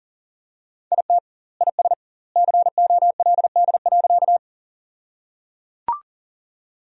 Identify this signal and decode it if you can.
Morse code, heard as a keyed tone, about 37 wpm, 700 Hz. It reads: IT IS COLD.
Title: 2x Word-Spacing - Sentences from Top 400 Words - 40wpm